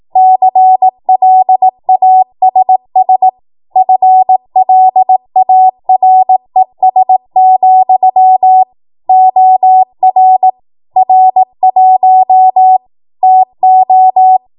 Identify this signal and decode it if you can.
Morse code, heard as a keyed tone, about 18 wpm, 750 Hz. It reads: CLASS FLARES, OR R1 TO
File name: Morse / 18WPM